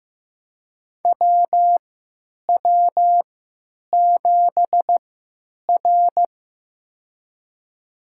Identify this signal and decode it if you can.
Morse code, heard as a keyed tone, about 15 wpm, spaced as for 9 wpm, 700 Hz. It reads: WW7R
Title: USA Call Signs - 4 Characters - 3x Character Spacing - 15wpm